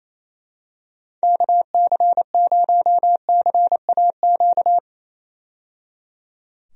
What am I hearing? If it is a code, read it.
Morse code, heard as a keyed tone, about 27 wpm, 700 Hz. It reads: KC0CAQ
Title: USA Call Signs - 6 Characters - 28wpm